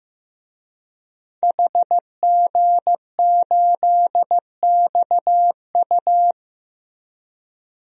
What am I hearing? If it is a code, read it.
Morse code, heard as a keyed tone, about 16 wpm, 700 Hz. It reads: HG8XU
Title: DXCC-100 Call Signs - 5 Characters - Repeated 3x - 15wpm